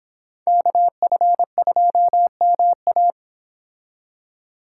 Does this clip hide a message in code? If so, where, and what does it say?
Yes: Morse code from 0.5 to 3.1 s, heard as a keyed tone, about 32 wpm, 700 Hz. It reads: KF2MA